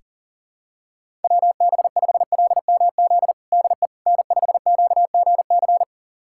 Morse code, heard as a keyed tone, about 40 wpm, 700 Hz. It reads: WB5LMZ DE N5QGC
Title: USA Callsign DE Callsign 40wpm